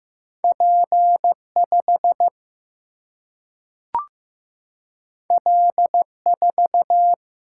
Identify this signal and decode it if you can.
Morse code, heard as a keyed tone, about 15 wpm, 700 Hz. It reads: P5 L4
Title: Single-Letter Single-Number 15wpm